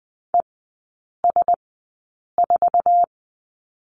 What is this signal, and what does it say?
Morse code, heard as a keyed tone, about 20 wpm, spaced as for 8 wpm, 700 Hz. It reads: ES4
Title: 047 Lesson 9e - Number 4 - Three Characters 20wpm